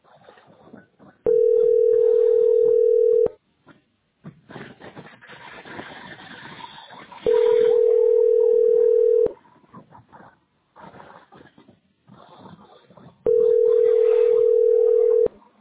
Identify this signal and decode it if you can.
Ringback tone (North America), 440 + 480 Hz.